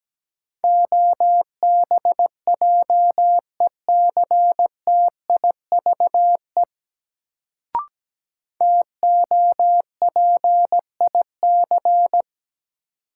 Morse code, heard as a keyed tone, about 17 wpm, 700 Hz. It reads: OBJECTIVE TOPIC
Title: Sentences from Top 1200 Words - New Word Drill - 17wpm